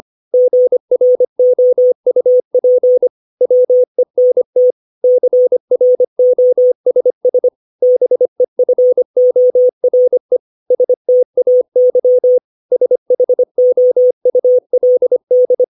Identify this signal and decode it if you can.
Morse code, heard as a keyed tone, about 25 wpm, 500 Hz. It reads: GROUP WENT CROSS BEFORE STAY SHOULD
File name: 632 words / 632 Words 25WPM